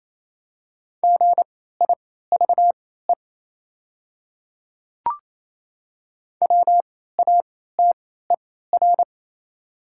Morse code, heard as a keyed tone, about 28 wpm, spaced as for 16 wpm, 700 Hz. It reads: GIVE WATER
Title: 3x Character Spacing - Top 300 Words - 28wpm